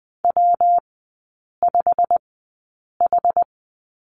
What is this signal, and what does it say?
Morse code, heard as a keyed tone, about 20 wpm, spaced as for 8 wpm, 700 Hz. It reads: W5H
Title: 138 Lesson 23e - Letter f - Three Characters 20wpm